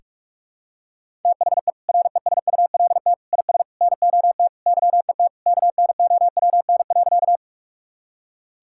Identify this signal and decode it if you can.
Morse code, heard as a keyed tone, about 46 wpm, 700 Hz. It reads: THE RESULT IS NOT YET KNOWN.
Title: Sentences from Top 1000 Words - Review Part 5 - 45wpm